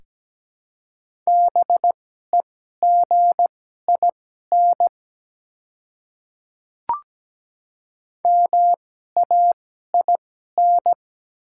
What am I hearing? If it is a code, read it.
Morse code, heard as a keyed tone, about 17 wpm, spaced as for 12 wpm, 700 Hz. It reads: BEGIN MAIN
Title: 2x Character Spacing - Top 300 Words - 17wpm